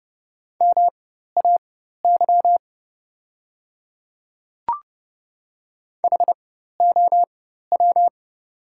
Morse code, heard as a keyed tone, about 30 wpm, spaced as for 14 wpm, 690 Hz. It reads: MAY HOW
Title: 4x Character Spacing - Top 100 Words - 30wpm